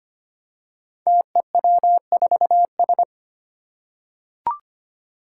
Morse code, heard as a keyed tone, about 25 wpm, 700 Hz. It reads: TEW4S
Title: Five Challenging Letter-Number Combinations 25wpm